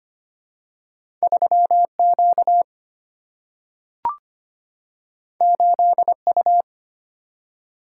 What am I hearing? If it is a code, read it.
Morse code, heard as a keyed tone, about 25 wpm, 700 Hz. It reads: 3Q 8U
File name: Two Letter-Number Combinations - 25wpm